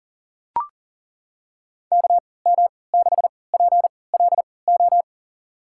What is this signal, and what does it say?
Morse code, heard as a keyed tone, about 40 wpm, spaced as for 23 wpm, 700 Hz. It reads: KM6PLO
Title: USA Call Signs - 6 Characters - 3x Character Spacing - 40wpm